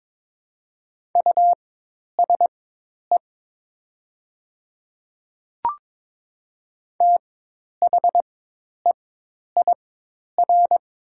Morse code, heard as a keyed tone, about 22 wpm, spaced as for 10 wpm, 700 Hz. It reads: USE THEIR
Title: 4x Character Spacing - Top 100 Words - 22wpm